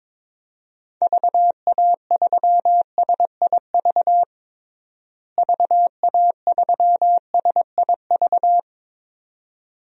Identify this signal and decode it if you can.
Morse code, heard as a keyed tone, about 22 wpm, 700 Hz. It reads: VA3SIV VA3SIV
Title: DXCC-100 Call Signs - 6 Characters - Repeated 2x - 22wpm